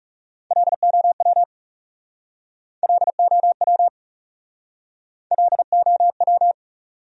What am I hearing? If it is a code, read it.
Morse code, heard as a keyed tone, about 38 wpm, 700 Hz. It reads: LOW LOW LOW